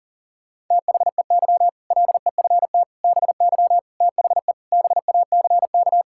Morse code, heard as a keyed tone, about 40 wpm, 700 Hz. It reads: THEY LEFT BY THE BACK
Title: Sentences from Top 1600 Words 40wpm